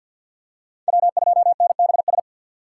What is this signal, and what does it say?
Morse code, heard as a keyed tone, about 51 wpm, 700 Hz. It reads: W2NBS
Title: USA Call Signs - 4x Character Spacing - 50wpm